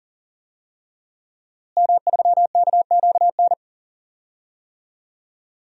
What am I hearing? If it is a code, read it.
Morse code, heard as a keyed tone, about 39 wpm, 700 Hz. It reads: M3KQN
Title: DXCC-100 Call Signs - 5 Characters - Repeated 3x - 40wpm